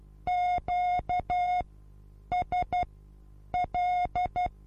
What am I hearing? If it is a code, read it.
Morse code, heard as a keyed tone, about 12 wpm, spaced as for 8 wpm, 720 Hz. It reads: QSL
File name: Onlinelehrgang / morsen / DJ4UF Morsen23